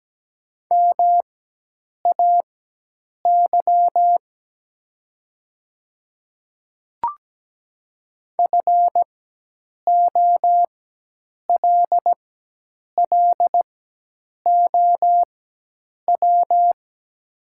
Morse code, heard as a keyed tone, about 17 wpm, spaced as for 8 wpm, 700 Hz. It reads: MAY FOLLOW
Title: 4x Character Spacing - Top 300 Words - 17wpm